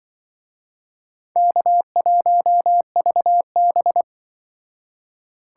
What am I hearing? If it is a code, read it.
Morse code, heard as a keyed tone, about 25 wpm, 700 Hz. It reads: K1VB
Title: Speed Racing - USA 1x2 Call Signs 20wpm